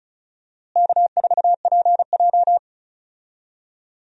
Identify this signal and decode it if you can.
Morse code, heard as a keyed tone, about 34 wpm, 700 Hz. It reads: K4PJ
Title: USA Call Signs - 4 Characters - Repeated 3x - 35wpm